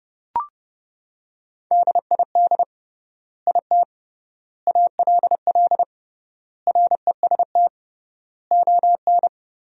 Morse code, heard as a keyed tone, about 29 wpm, 700 Hz. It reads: DID IT ALL REST ON
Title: 3x Word-Spacing - Sentences from Top 500 Words - 30wpm